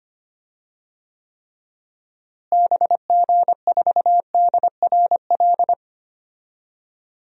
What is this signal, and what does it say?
Morse code, heard as a keyed tone, about 25 wpm, 700 Hz. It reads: BG4DRL
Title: DXCC-100 Call Signs - 6 Characters - Repeated 3x - 25wpm